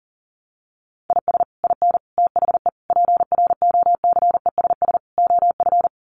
Morse code, heard as a keyed tone, about 40 wpm, 700 Hz. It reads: IS IN THE PROCESS OF
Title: Sets of 5 Words Encore2 40wpm